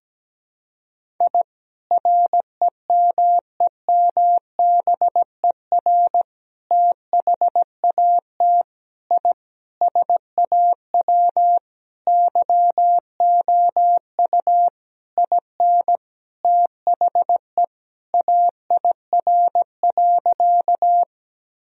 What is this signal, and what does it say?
Morse code, heard as a keyed tone, about 17 wpm, 700 Hz. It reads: I REMEMBER THAT I SAW YOU IN THE AIR.
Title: Sentences from Top 1000 Words - Review Part 1 - 17wpm